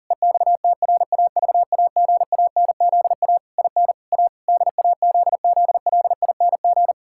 Morse code, heard as a keyed tone, about 37 wpm, 700 Hz. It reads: EXTRAVAGANZA IN A DAZZLING
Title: Pangram Challenge v2 - 40wpm